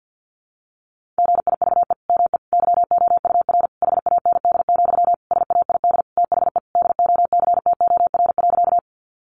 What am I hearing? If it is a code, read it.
Morse code, heard as a keyed tone, about 50 wpm, 700 Hz. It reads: GIVE ME YOUR HAND, SAID THE DOCTOR.
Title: Sentences from Top 1500 Words 50wpm